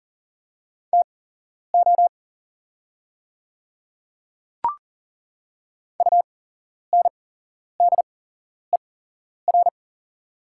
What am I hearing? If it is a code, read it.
Morse code, heard as a keyed tone, about 40 wpm, spaced as for 11 wpm, 690 Hz. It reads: TO UNDER
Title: 8x Character Spacing - Top 200 Words - 40wpm